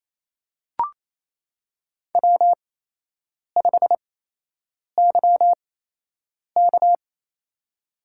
Morse code, heard as a keyed tone, about 28 wpm, spaced as for 8 wpm, 700 Hz. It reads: W5YK
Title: USA Call Signs - 4 Characters - 8x Character Spacing - 28wpm